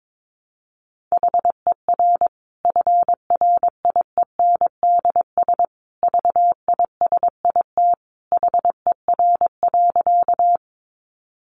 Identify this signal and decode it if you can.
Morse code, heard as a keyed tone, about 22 wpm, 700 Hz. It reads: HER FRIENDS VISIT HER.
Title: Easy Phrases 22wpm